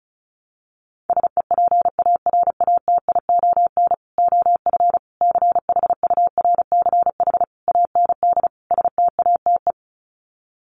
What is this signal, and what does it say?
Morse code, heard as a keyed tone, about 34 wpm, 700 Hz. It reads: SEPARATION OF CHURCH AND STATE
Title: Sets of 5 Words 35wpm